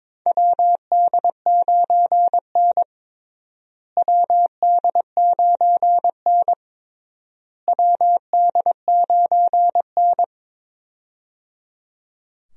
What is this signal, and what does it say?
Morse code, heard as a keyed tone, about 22 wpm, 700 Hz. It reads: WD9N WD9N WD9N
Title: USA Call Signs - 4 Characters - Repeated 3x - 22wpm